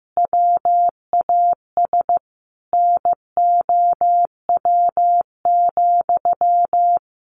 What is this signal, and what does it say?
Morse code, heard as a keyed tone, about 15 wpm, 700 Hz. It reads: WAS NOW,
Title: A Sound Of Thunder 15wpm